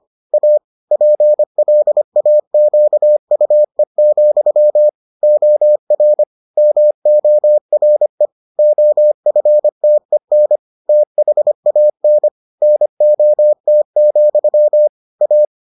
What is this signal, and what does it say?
Morse code, heard as a keyed tone, about 25 wpm, 600 Hz. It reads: A PLAQUE, OR MORE OFTEN THAN NOT, A